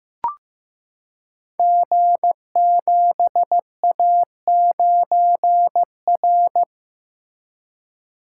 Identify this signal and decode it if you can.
Morse code, heard as a keyed tone, about 15 wpm, 700 Hz. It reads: G7A9R
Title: Code Groups - 5 Letter:Numbers - 15wpm